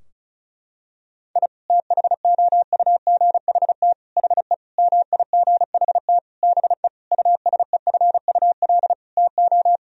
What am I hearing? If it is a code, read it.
Morse code, heard as a keyed tone, about 36 wpm, 700 Hz. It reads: I THOUGHT HE MIGHT BE USEFUL TO
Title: Sentences from Top 1000 Words - Review Part 4 - 35wpm